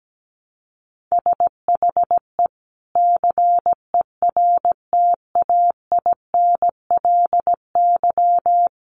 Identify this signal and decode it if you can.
Morse code, heard as a keyed tone, about 17 wpm, 700 Hz. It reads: SHE CERTAINLY
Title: Sentences from Top 1100 Words 17wpm